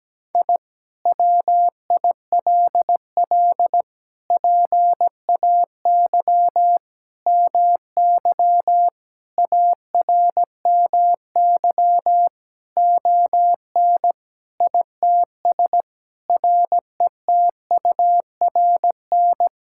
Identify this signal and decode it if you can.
Morse code, heard as a keyed tone, about 17 wpm, 700 Hz. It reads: I WILL PAY MY ARMY ON ITS RETURN
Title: Sentences from Top 1000 Words - Review Part 5 - 17wpm